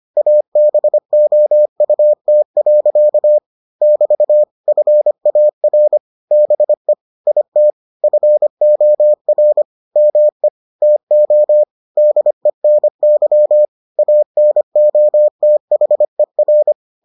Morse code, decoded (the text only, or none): ABOUT. <BT> FAR BE IT FOR ME TO DENY ANOTHER